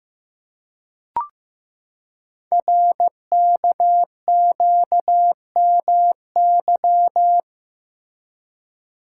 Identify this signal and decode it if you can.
Morse code, heard as a keyed tone, about 15 wpm, 700 Hz. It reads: RKQMY
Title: Code Groups - 5 Letter:Numbers - 15wpm